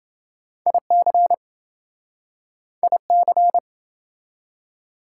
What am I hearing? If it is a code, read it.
Morse code, heard as a keyed tone, about 27 wpm, 700 Hz. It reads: IC IC